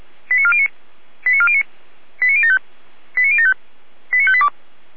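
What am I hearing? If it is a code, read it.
Five-tone selective call (ZVEI), heard as quick runs of five tones, tones of about 70 ms. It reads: 89409 89409 89075 89075 89582
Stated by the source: ZVEI five-tone selective call received by SDR.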